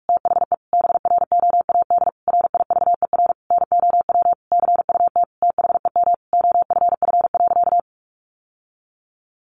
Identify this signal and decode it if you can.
Morse code, heard as a keyed tone, about 44 wpm, 700 Hz. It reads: THE BROAD RIVER NOW CUT THEM OFF.
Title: Sentences from Top 1000 Words 45wpm